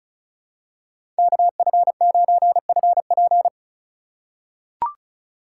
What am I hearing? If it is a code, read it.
Morse code, heard as a keyed tone, about 36 wpm, 700 Hz. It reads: KF9FP
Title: USA Call Signs - 4x Character Spacing - 35wpm